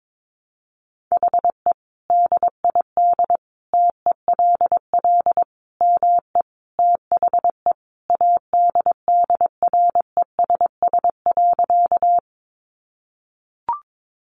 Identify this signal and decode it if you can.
Morse code, heard as a keyed tone, about 22 wpm, 700 Hz. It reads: HE DID TELL ME THE ADDRESS.
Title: Sentences from Top 1000 Words 22wpm